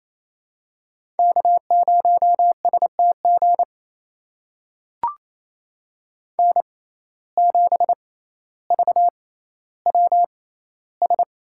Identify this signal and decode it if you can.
Morse code, heard as a keyed tone, about 28 wpm, spaced as for 14 wpm, 700 Hz. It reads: K0STG N7VWS
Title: USA Call Signs - 5 Characters - 6x Character Spacing - 28wpm